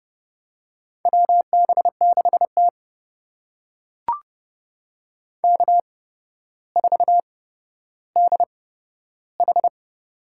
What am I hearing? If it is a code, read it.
Morse code, heard as a keyed tone, about 30 wpm, spaced as for 13 wpm, 700 Hz. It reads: WB6T K4DH
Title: USA Call Signs - 4 Characters - 8x Character Spacing - 30wpm